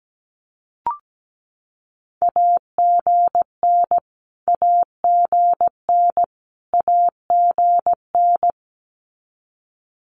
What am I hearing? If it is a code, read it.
Morse code, heard as a keyed tone, about 17 wpm, 700 Hz. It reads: AGN AGN AGN
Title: Abbreviations Repeated 3x - 17wpm